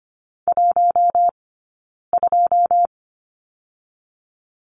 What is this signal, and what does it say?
Morse code, heard as a keyed tone, about 25 wpm, 700 Hz. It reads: 1 2
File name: course / 25:10wpm / 092 Lesson 16d - Letter c - Two Characters 25wpm